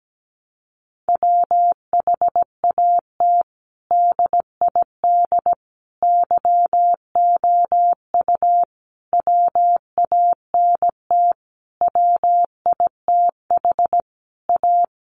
Morse code, heard as a keyed tone, about 17 wpm, 700 Hz. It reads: WHAT DID YOU WANT WITH A
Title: Sentences from Top 1400 Words 17wpm